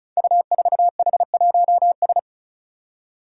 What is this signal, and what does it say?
Morse code, heard as a keyed tone, about 34 wpm, 700 Hz. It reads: U4H1S
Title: Five Challenging Letter-Number Combinations 35wpm